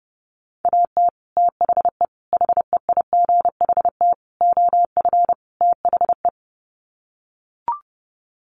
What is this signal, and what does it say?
Morse code, heard as a keyed tone, about 30 wpm, 700 Hz. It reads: AT THE HEIGHT OF THE
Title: Sets of 5 Words 30wpm